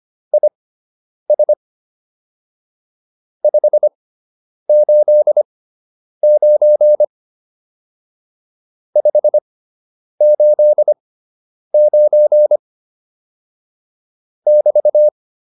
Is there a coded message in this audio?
Yes: Morse code from 0.3 to 15.1 s, heard as a keyed tone, about 25 wpm, spaced as for 9 wpm, 600 Hz. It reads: IS 589 589 <BT>